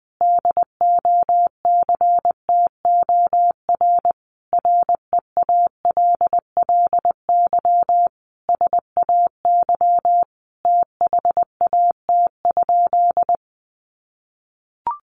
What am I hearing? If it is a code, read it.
Morse code, heard as a keyed tone, about 19 wpm, 700 Hz. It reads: DOCTOR REALLY SAY THAT?
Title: Sentences from Top 1500 Words 20wpm